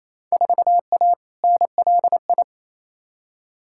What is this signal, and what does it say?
Morse code, heard as a keyed tone, about 28 wpm, 700 Hz. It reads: 4A NLI
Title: Field Day Exchanges Encore - 28wpm